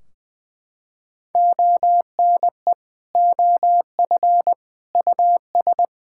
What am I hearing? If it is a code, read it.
Morse code, heard as a keyed tone, about 20 wpm, 700 Hz. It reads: ONE OF US
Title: Sentences from Top 1600 Words 20wpm